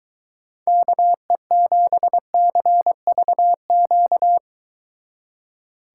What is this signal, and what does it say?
Morse code, heard as a keyed tone, about 23 wpm, 700 Hz. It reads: KE7CVQ